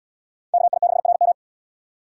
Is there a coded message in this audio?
Yes: Morse code from 0.5 to 1.3 s, heard as a keyed tone, about 72 wpm, 700 Hz. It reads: KI6RM